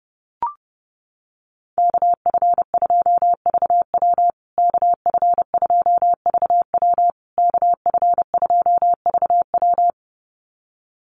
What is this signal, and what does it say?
Morse code, heard as a keyed tone, about 30 wpm, 700 Hz. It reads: KF2VW KF2VW KF2VW
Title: USA Call Signs Repeated 3x 30wpm